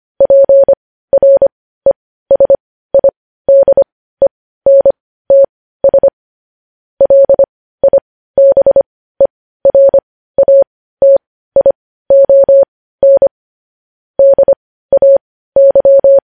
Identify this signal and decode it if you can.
Morse code, heard as a keyed tone, about 25 wpm, spaced as for 15 wpm, 550 Hz. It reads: PRESIDENTS LIBERATION DAY
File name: podcast / News headlines-1x-25x15